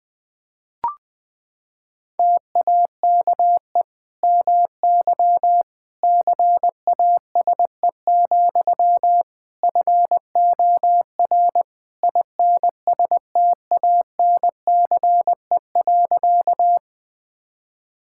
Morse code, heard as a keyed tone, about 20 wpm, 700 Hz. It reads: TAKE MY CASE, FOR INSTANCE.
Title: Sentences from Top 1000 Words - Review Part 5 - 20wpm